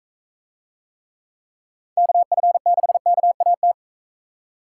Morse code, heard as a keyed tone, about 41 wpm, 700 Hz. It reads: KF6KAT